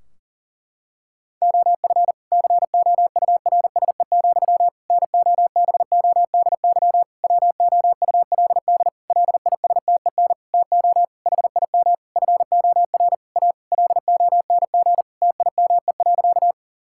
Morse code, heard as a keyed tone, about 40 wpm, 700 Hz. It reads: OF COURSE, NOBODY WOULD LISTEN TO HIM FOR A LONG TIME.
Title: Sentences from Top 1600 Words 40wpm